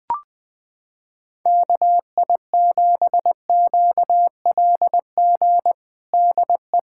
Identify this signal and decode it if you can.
Morse code, heard as a keyed tone, about 20 wpm, 700 Hz. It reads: KI7QLG DE
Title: USA Callsign DE Callsign - 6 Characters - 20wpm